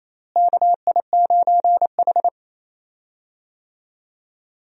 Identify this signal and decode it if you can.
Morse code, heard as a keyed tone, about 28 wpm, 700 Hz. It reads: KI9H
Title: USA Call Signs - 4 Characters - Repeated 2x - 28wpm